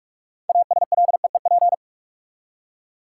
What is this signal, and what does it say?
Morse code, heard as a keyed tone, about 47 wpm, 700 Hz. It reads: ASLEEP
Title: Six Letter Words - 45wpm